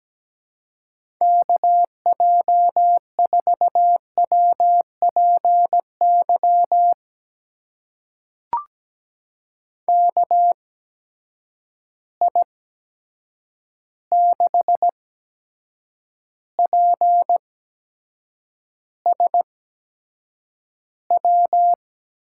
Morse code, heard as a keyed tone, about 17 wpm, spaced as for 7 wpm, 700 Hz. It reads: KJ4WPY KI6PSW
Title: USA Call Signs - 6 Characters - 8x Character Spacing - 17wpm